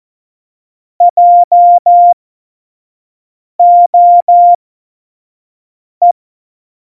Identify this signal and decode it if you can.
Morse code, heard as a keyed tone, about 14 wpm, spaced as for 5 wpm, 700 Hz. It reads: JOE
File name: Morsekurs / Witze / Witz 26. Dezember 2017, 25 F70